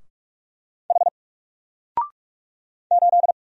Morse code, heard as a keyed tone, about 44 wpm, 700 Hz. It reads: H 8
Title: Single Letter-Number Alt 45wpm